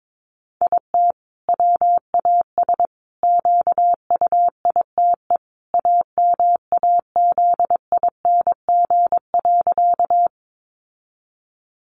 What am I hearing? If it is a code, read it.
Morse code, heard as a keyed tone, about 22 wpm, 700 Hz. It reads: IT WAS QUITE AMAZING.
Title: Sentences from Top 1500 Words 22wpm